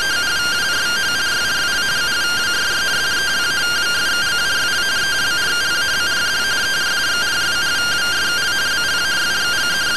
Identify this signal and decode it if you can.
Two-tone teleprinter signal, about 170 Hz shift.